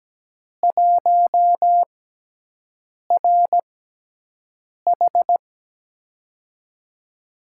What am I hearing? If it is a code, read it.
Morse code, heard as a keyed tone, about 17 wpm, 700 Hz. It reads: 1 R H
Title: USA Call Signs - 5 Characters - 6x Character Spacing - 17wpm